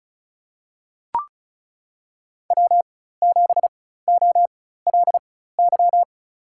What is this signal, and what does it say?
Morse code, heard as a keyed tone, about 36 wpm, spaced as for 16 wpm, 700 Hz. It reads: W7OLY
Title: USA Call Signs - 5 Characters - 4x Character Spacing - 35wpm